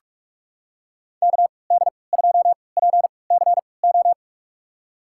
Morse code, heard as a keyed tone, about 46 wpm, spaced as for 25 wpm, 700 Hz. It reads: KD2PCO
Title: USA Call Signs - 6 Characters - 3x Character Spacing - 45wpm